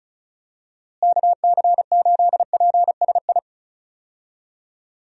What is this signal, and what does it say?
Morse code, heard as a keyed tone, about 36 wpm, 700 Hz. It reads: KC8PSI
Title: USA Call Signs - 6 Characters - Repeated 3x - 35wpm